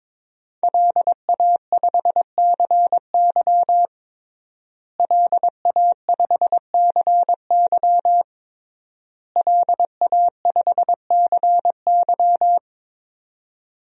Morse code, heard as a keyed tone, about 22 wpm, 700 Hz. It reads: LA5CY LA5CY LA5CY
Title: DXCC-100 Call Signs - 5 Characters - Repeated 3x - 22wpm